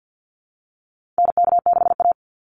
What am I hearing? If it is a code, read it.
Morse code, heard as a keyed tone, about 48 wpm, 700 Hz. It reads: NK6A
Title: NA SOTA Chaser Callsigns 50wpm